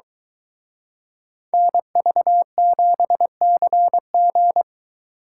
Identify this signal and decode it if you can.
Morse code, heard as a keyed tone, about 22 wpm, 700 Hz. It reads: NV7CG